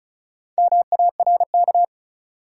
Morse code, heard as a keyed tone, about 35 wpm, 700 Hz. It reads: MARK
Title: Sentences from Top 300 Words - New Word Drill - 35wpm